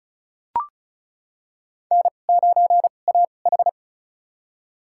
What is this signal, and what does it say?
Morse code, heard as a keyed tone, about 36 wpm, spaced as for 25 wpm, 690 Hz. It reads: N9AH
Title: USA Call Signs - 4 Characters - 2x Character Spacing - 35wpm